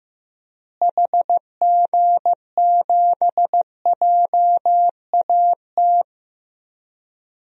Morse code, heard as a keyed tone, about 15 wpm, 700 Hz. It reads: HG7JAT
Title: DXCC-100 Call Signs - 6 Characters - Repeated 2x - 15wpm